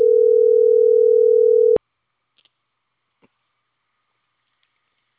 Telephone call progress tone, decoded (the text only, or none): ringback tone